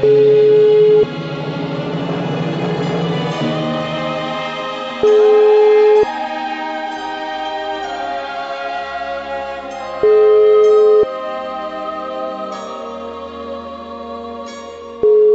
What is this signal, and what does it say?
Ringback tone (Europe, 425 Hz), with noise.